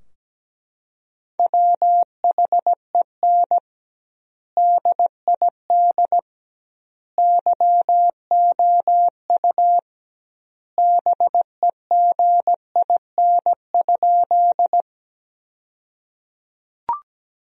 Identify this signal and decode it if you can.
Morse code, heard as a keyed tone, about 17 wpm, 700 Hz. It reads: WHEN DID YOU BEGIN?
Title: 2x Word-Spacing - Sentences from Top 300 Words - 17wpm